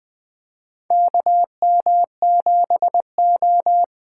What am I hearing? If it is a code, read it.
Morse code, heard as a keyed tone, about 20 wpm, 700 Hz. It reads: KM7O
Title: USA Call Signs - 6x Character Spacing - 20wpm